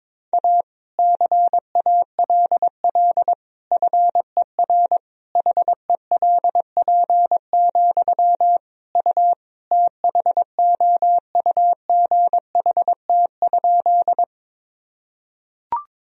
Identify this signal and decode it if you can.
Morse code, heard as a keyed tone, about 21 wpm, 700 Hz. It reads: A CALL FER HELP, U THOUGHT?
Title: Sentences from Top 300 Words and Abbreviations 22wpm